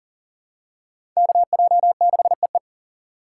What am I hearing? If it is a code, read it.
Morse code, heard as a keyed tone, about 40 wpm, 700 Hz. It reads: KJ6EE
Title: USA Call Signs - 2x Character Spacing - 40wpm